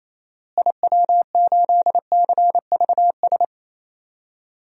Morse code, heard as a keyed tone, about 28 wpm, 700 Hz. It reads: IW8CVS
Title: DXCC-100 Call Signs - 6 Characters - Repeated 2x - 28wpm